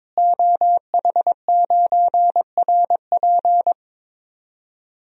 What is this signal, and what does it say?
Morse code, heard as a keyed tone, about 21 wpm, 700 Hz. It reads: OH9RP
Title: DXCC-100 Call Signs - 5 Characters - Repeated 2x - 22wpm